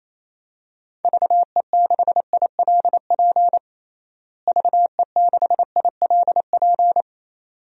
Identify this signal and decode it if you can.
Morse code, heard as a keyed tone, about 28 wpm, 700 Hz. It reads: VE6ILP VE6ILP
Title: DXCC-100 Call Signs - 6 Characters - Repeated 2x - 28wpm